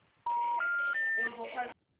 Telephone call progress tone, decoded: special information tone

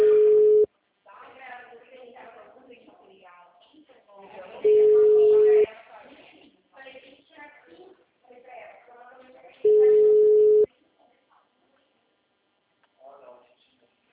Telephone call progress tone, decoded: ringback tone

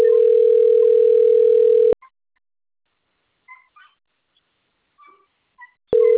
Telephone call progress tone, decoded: ringback tone

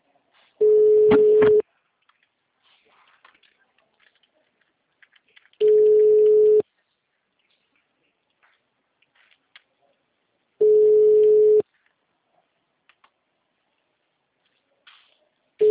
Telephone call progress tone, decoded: ringback tone

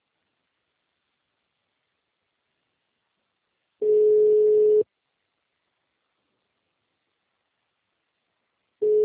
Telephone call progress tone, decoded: ringback tone

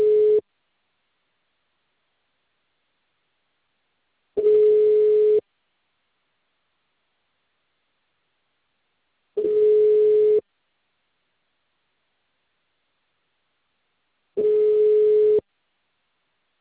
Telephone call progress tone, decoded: ringback tone

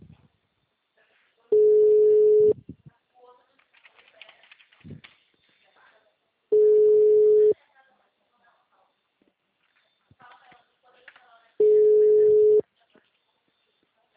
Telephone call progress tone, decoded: ringback tone

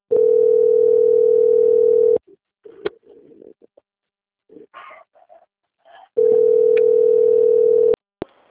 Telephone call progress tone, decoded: ringback tone

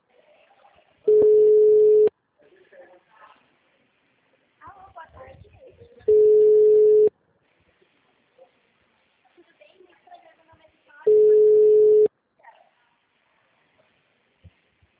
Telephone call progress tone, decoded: ringback tone

